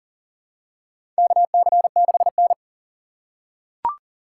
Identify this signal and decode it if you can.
Morse code, heard as a keyed tone, about 40 wpm, 700 Hz. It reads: KC6N
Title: USA Call Signs - 2x Character Spacing - 40wpm